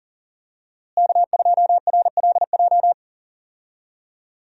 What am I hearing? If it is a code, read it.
Morse code, heard as a keyed tone, about 37 wpm, 700 Hz. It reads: K2RLJ
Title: USA Call Signs - 5 Characters - Repeated 2x - 40wpm